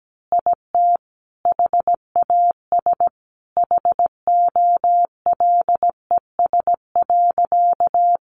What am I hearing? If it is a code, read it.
Morse code, heard as a keyed tone, about 17 wpm, 700 Hz. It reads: IT HAS HOLES.
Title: Additional Easy Phrases 17wpm